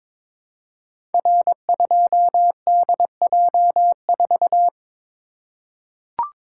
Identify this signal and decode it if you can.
Morse code, heard as a keyed tone, about 22 wpm, 700 Hz. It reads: R2DJ4
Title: Code Groups - 5 Letter:Numbers - 22wpm